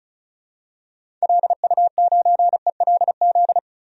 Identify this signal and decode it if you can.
Morse code, heard as a keyed tone, about 36 wpm, 700 Hz. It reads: LU9ELZ
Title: DXCC-100 Call Signs - 6 Characters - 35wpm